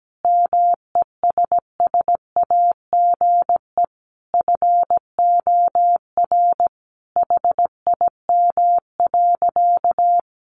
Morse code, heard as a keyed tone, about 17 wpm, 700 Hz. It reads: MESSAGE FOR HIM.
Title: Sentences from Top 1100 Words 17wpm